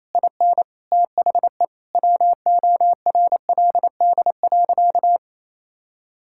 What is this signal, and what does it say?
Morse code, heard as a keyed tone, about 28 wpm, 700 Hz. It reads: IN THE WORLD.
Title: Sentences from Top 1000 Words - Review Part 6 - 28wpm